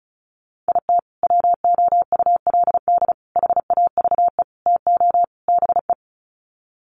Morse code, heard as a keyed tone, about 34 wpm, 700 Hz. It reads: IT WOULD HAVE TO BE
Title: Sets of 5 Words Encore 35wpm